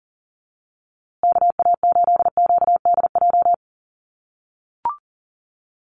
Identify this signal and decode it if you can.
Morse code, heard as a keyed tone, about 40 wpm, 700 Hz. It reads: KA8QDJ
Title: USA Call Signs Repeated 3x 40wpm